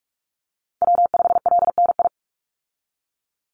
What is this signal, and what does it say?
Morse code, heard as a keyed tone, about 46 wpm, 700 Hz. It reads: W5LNI
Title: Call Signs - 45wpm